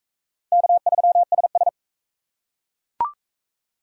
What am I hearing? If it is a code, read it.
Morse code, heard as a keyed tone, about 34 wpm, 700 Hz. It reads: K3SS
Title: Speed Racing - USA 1x2 Call Signs 28wpm